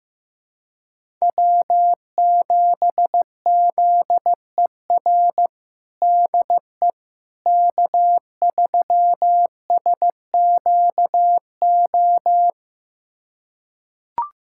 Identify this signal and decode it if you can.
Morse code, heard as a keyed tone, about 15 wpm, 700 Hz. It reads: W7ZER DE K3SQO
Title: USA Callsign DE Callsign - 5 Characters - 15wpm